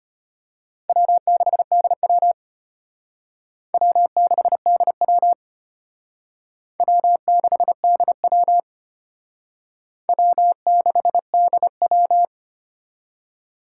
Morse code, heard as a keyed tone, about 32 wpm, 700 Hz. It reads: W6DW W6DW W6DW W6DW